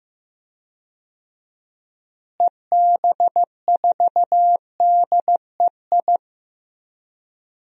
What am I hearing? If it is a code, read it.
Morse code, heard as a keyed tone, about 16 wpm, 700 Hz. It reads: EB4DEI
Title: DXCC-100 Call Signs - 6 Characters - Repeated 3x - 15wpm